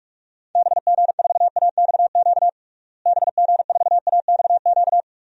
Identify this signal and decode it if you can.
Morse code, heard as a keyed tone, about 46 wpm, 700 Hz. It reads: BG4AXQ BG4AXQ
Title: DXCC-100 Call Signs - 6 Characters - Repeated 2x - 45wpm